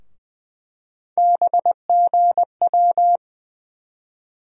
Morse code, heard as a keyed tone, about 20 wpm, 700 Hz. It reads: BGW
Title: USA Call Sign Postfixes - 3 Letters 20wpm